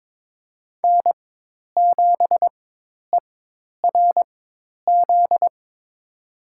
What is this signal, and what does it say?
Morse code, heard as a keyed tone, about 21 wpm, spaced as for 10 wpm, 700 Hz. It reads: N7ERZ
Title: USA Call Signs - 5 Characters - 4x Character Spacing - 22wpm